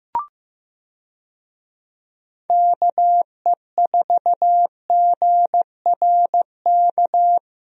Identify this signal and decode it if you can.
Morse code, heard as a keyed tone, about 15 wpm, 700 Hz. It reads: KE4GRK
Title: USA Call Signs - 6 Characters - Repeated 2x - 15wpm